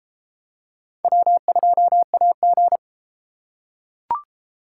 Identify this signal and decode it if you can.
Morse code, heard as a keyed tone, about 33 wpm, 700 Hz. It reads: W2AG